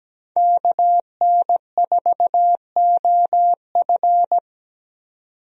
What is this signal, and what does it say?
Morse code, heard as a keyed tone, about 17 wpm, 700 Hz. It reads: KN4OF
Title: USA Call Signs - 5 Characters - 17wpm